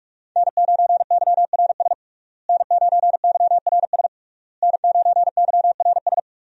Morse code, heard as a keyed tone, about 46 wpm, 700 Hz. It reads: N9YRS N9YRS N9YRS
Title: USA Call Signs - 5 Characters - Repeated 3x - 45wpm